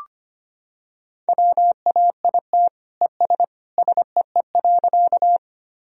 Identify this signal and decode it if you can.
Morse code, heard as a keyed tone, about 25 wpm, 700 Hz. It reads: WAIT ES SEE.
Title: Sentences from Top 500 Words and Abbreviations 25wpm